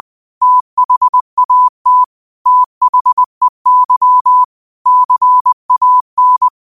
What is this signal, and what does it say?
Morse code, heard as a keyed tone, about 20 wpm, 1000 Hz. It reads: THAT THEY CAN
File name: qotdfiles / 20wpm